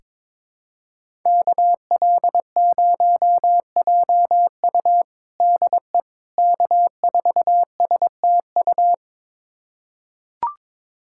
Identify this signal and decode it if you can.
Morse code, heard as a keyed tone, about 22 wpm, 700 Hz. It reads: KL0JU DE K4STU
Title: USA Callsign DE Callsign - 5 Characters - 22wpm